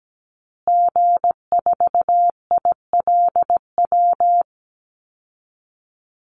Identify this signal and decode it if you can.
Morse code, heard as a keyed tone, about 17 wpm, 700 Hz. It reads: G4ILW
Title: Call Signs Repeated 2x - 17wpm